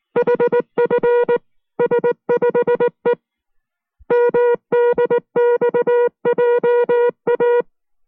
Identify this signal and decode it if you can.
Morse code, heard as a keyed tone, about 19 wpm, 480 Hz. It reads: HFS5E MDXJA